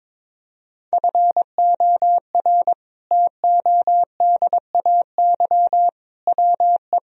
Morse code, heard as a keyed tone, about 22 wpm, 700 Hz. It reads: FOR TODAY WE
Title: Sentences from Top 1000 Words - Review Part 1 - 22wpm